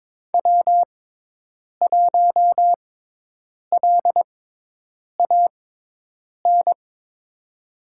Morse code, heard as a keyed tone, about 22 wpm, spaced as for 8 wpm, 700 Hz. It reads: W1LAN
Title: USA Call Signs - 5 Characters - 6x Character Spacing - 22wpm